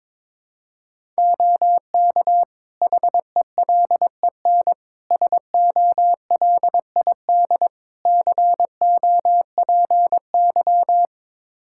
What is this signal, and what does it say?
Morse code, heard as a keyed tone, about 22 wpm, 700 Hz. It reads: OK HELEN SOLID COPY
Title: Practice QSO - Exchange 3 - More Info - 22wpm